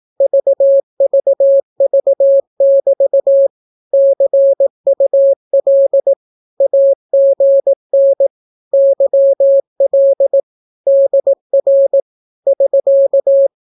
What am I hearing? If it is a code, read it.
Morse code, heard as a keyed tone, about 18 wpm, 550 Hz. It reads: VVV<BT> CUL AGN YL DR <SK>
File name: naukapodstaw / skroty / skroty